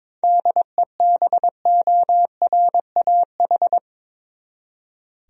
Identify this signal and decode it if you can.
Morse code, heard as a keyed tone, about 22 wpm, 700 Hz. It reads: DEBORAH
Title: USA Common Names - 22wpm